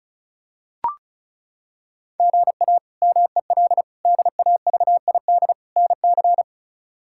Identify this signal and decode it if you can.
Morse code, heard as a keyed tone, about 36 wpm, 700 Hz. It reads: GA MEL DAVID NC
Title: K1USN SST QSO Practice - 35wpm